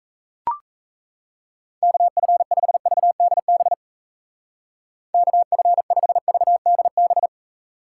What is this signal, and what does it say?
Morse code, heard as a keyed tone, about 37 wpm, 700 Hz. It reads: KF5VDB KF5VDB